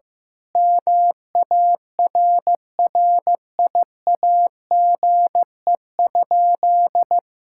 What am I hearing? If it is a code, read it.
Morse code, heard as a keyed tone, about 16 wpm, 700 Hz. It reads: MARRIAGE?
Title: Sentences from Top 1000 Words - Review Part 5 - 15wpm